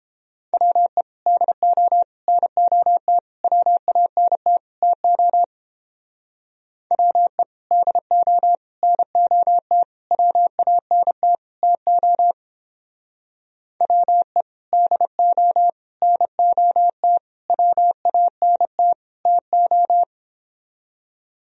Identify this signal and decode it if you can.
Morse code, heard as a keyed tone, about 31 wpm, 700 Hz. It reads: WE DO NOT WANT TO WE DO NOT WANT TO WE DO NOT WANT TO